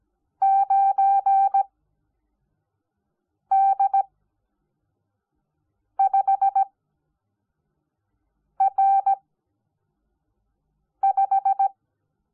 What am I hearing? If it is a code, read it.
Morse code, heard as a keyed tone, about 17 wpm, spaced as for 4 wpm, 780 Hz. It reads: 9D5R5
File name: Onlinelehrgang / morsen / DJ4UF Morsen05